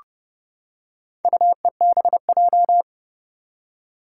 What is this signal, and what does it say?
Morse code, heard as a keyed tone, about 30 wpm, 700 Hz. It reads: UEBJ
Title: Four Challenging Letter-Number Combinations 30wpm